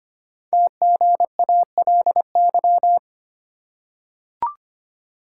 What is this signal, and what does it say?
Morse code, heard as a keyed tone, about 25 wpm, 700 Hz. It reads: TGALY
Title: Code Groups - 5 Letters - 25wpm